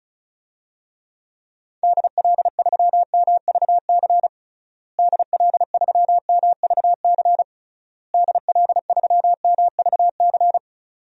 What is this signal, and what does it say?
Morse code, heard as a keyed tone, about 36 wpm, 700 Hz. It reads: DL3MVC DL3MVC DL3MVC
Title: DXCC-100 Call Signs - 6 Characters - Repeated 3x - 35wpm